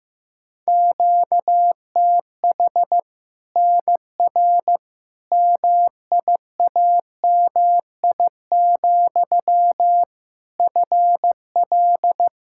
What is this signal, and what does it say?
Morse code, heard as a keyed tone, about 15 wpm, 700 Hz. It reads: QTH NR MIAMI, FL
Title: Practice QSO - Exchange 1 - Beginning - 15wpm